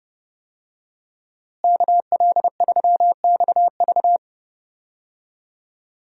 Morse code, heard as a keyed tone, about 32 wpm, 700 Hz. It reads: KL3XV
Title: USA Call Signs - 5 Characters - Repeated 3x - 30wpm